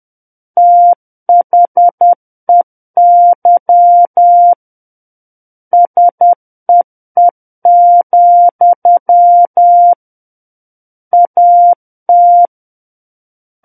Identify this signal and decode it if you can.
Morse code, heard as a keyed tone, about 10 wpm, 700 Hz. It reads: THEY SEE, AT
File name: Book One / 01-Chapter 1 - 10 WPM